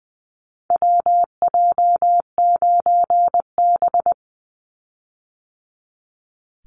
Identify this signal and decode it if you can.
Morse code, heard as a keyed tone, about 20 wpm, 700 Hz. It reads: WJ9B
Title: Call Signs - 20wpm